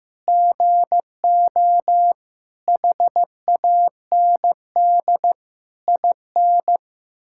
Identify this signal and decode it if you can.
Morse code, heard as a keyed tone, about 16 wpm, 700 Hz. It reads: GO HAND IN
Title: Sentences from Top 1000 Words - Review Part 5 - 15wpm